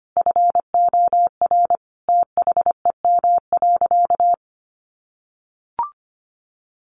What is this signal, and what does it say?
Morse code, heard as a keyed tone, about 25 wpm, 700 Hz. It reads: FOR THEM.
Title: Sentences from Top 1300 Words 25wpm